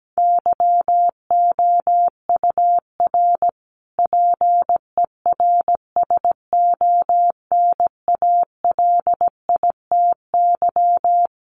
Morse code, heard as a keyed tone, about 17 wpm, 700 Hz. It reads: YOUR PERSONALITY
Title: Sentences from Top 1400 Words 17wpm